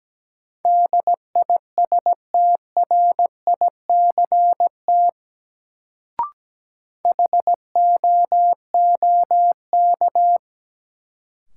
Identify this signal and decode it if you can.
Morse code, heard as a keyed tone, about 17 wpm, 700 Hz. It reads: DISTRICT HOOK
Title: Sentences from Top 1600 Words - New Word Drill - 17wpm